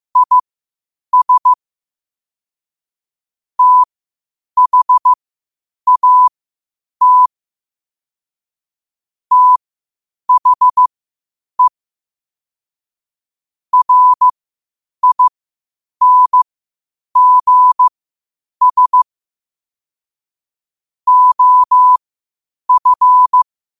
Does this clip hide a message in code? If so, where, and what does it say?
Yes: Morse code from 0.2 to 23.4 s, heard as a keyed tone, about 15 wpm, spaced as for 8 wpm, 1000 Hz. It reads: IS THAT THE RINGS OF